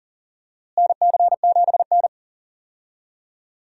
Morse code, heard as a keyed tone, about 39 wpm, 700 Hz. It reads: NC7N
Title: USA Call Signs - 4 Characters - Repeated 3x - 40wpm